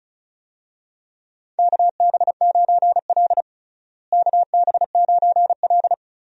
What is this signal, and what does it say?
Morse code, heard as a keyed tone, about 36 wpm, 700 Hz. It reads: KB9L KB9L
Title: USA Call Signs - 4 Characters - Repeated 2x - 35wpm